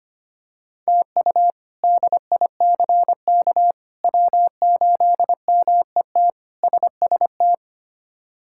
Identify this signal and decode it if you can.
Morse code, heard as a keyed tone, about 25 wpm, 700 Hz. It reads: TU DICK W8MET SST
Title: K1USN SST QSO Practice - 25wpm